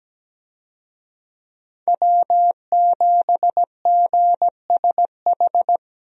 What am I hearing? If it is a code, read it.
Morse code, heard as a keyed tone, about 17 wpm, 700 Hz. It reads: W7GSH
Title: USA Call Signs - 5 Characters - Repeated 2x - 17wpm